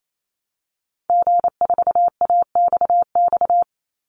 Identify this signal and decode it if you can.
Morse code, heard as a keyed tone, about 28 wpm, 700 Hz. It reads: G4AXX
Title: Call Signs - 28wpm